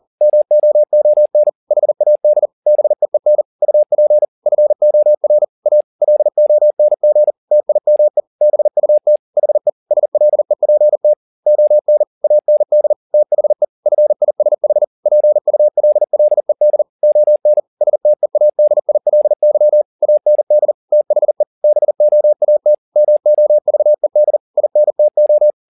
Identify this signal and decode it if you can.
Morse code, heard as a keyed tone, about 40 wpm, 600 Hz. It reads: MOON HAD BEEN UP FOR A LONG TIME BUT HE SLEPT ON AND THE FISH PULLED ON STEADILY AND THE BOAT MOVED INTO
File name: the old man and the sea 2 40wpm